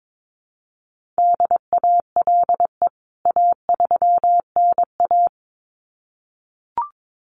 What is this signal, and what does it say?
Morse code, heard as a keyed tone, about 22 wpm, 700 Hz. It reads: DALE A3NA
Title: CWT Exchange 22wpm